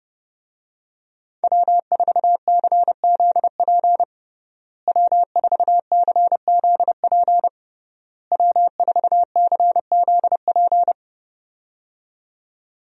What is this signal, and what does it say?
Morse code, heard as a keyed tone, about 30 wpm, 700 Hz. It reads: W4CZP W4CZP W4CZP
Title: USA Call Signs - 5 Characters - Repeated 3x - 30wpm